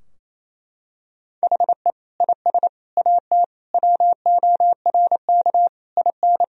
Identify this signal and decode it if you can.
Morse code, heard as a keyed tone, about 28 wpm, 700 Hz. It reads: HE IS AT WORK IN
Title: Sentences from Top 1000 Words - Review Part 4 - 28wpm